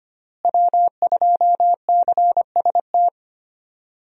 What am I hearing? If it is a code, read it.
Morse code, heard as a keyed tone, about 25 wpm, 700 Hz. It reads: W2CST
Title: USA Call Signs - 5 Characters - Repeated 2x - 25wpm